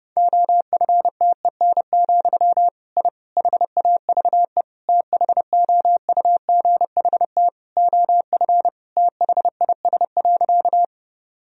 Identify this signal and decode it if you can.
Morse code, heard as a keyed tone, about 30 wpm, 700 Hz. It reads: OFTEN, I HAVE THOUGHT OF THIS.
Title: Sentences from Top 1000 Words - Review Part 1 - 30wpm